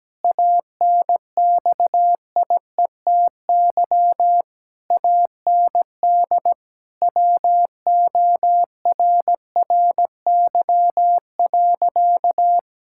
Morse code, heard as a keyed tone, about 17 wpm, 700 Hz. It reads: ANXIETY AND WORRY.
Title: Pangram Challenge v2 - 17wpm